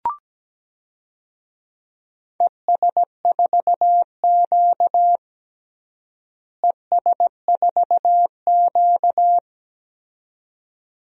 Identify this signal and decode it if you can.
Morse code, heard as a keyed tone, about 17 wpm, 700 Hz. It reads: ES4Q ES4Q
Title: DXCC-100 Call Signs - 4 Characters - Repeated 2x - 17wpm